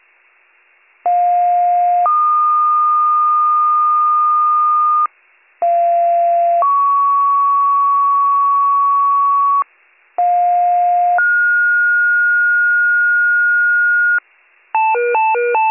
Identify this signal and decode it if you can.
Two-tone page, heard as a one-second tone, then a three-second tone, about 710 then 1220 Hz, about 690 then 1120 Hz, about 710 then 1430 Hz. It is heard from 1.1 to 14.2 s.